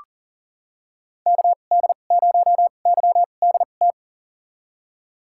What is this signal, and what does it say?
Morse code, heard as a keyed tone, about 39 wpm, spaced as for 29 wpm, 700 Hz. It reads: KD0YDT
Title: USA Call Signs - 6 Characters - 2x Character Spacing - 40wpm